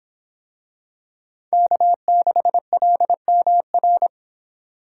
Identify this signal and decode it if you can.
Morse code, heard as a keyed tone, about 26 wpm, 700 Hz. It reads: K6LMR